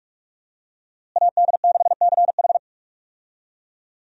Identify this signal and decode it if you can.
Morse code, heard as a keyed tone, about 42 wpm, 700 Hz. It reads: AD6CH
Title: USA Call Signs - 5 Characters - Repeated 2x - 45wpm